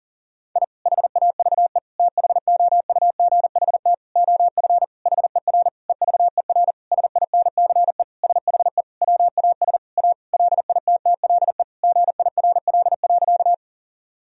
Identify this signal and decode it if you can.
Morse code, heard as a keyed tone, about 40 wpm, 700 Hz. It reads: I HAVE THOUGHT OF HER EVER SINCE SHE WAS A LITTLE GIRL.
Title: Sentences from Top 1000 Words - Review Part 1 - 40wpm